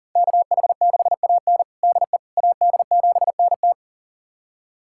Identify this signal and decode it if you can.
Morse code, heard as a keyed tone, about 40 wpm, 700 Hz. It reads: KH6AN DE AD7NT
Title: USA Callsign DE Callsign 40wpm